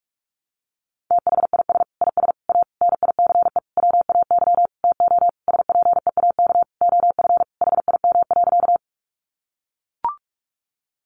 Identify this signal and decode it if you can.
Morse code, heard as a keyed tone, about 44 wpm, 700 Hz. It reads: THIS IS A NICE WAY TO SPEAK OF HIM.
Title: Sentences from Top 1100 Words 45wpm